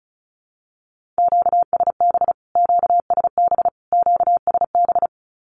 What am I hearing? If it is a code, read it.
Morse code, heard as a keyed tone, about 34 wpm, 700 Hz. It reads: QSB QSB QSB
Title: Q-Codes Repeated 3x 35wpm